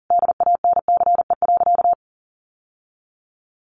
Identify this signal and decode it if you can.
Morse code, heard as a keyed tone, about 40 wpm, 700 Hz. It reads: DANCE.
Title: Sentences from Top 900 Words 40wpm